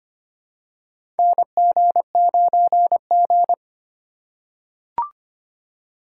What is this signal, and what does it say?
Morse code, heard as a keyed tone, about 25 wpm, 700 Hz. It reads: NG9G
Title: USA Call Signs - 4 Characters - 4x Character Spacing - 25wpm